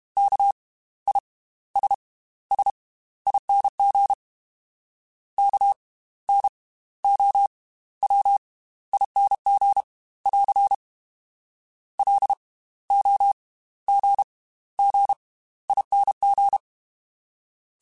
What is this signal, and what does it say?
Morse code, heard as a keyed tone, about 32 wpm, spaced as for 16 wpm, 800 Hz. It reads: KISSING KNOWING <AR> LOGGING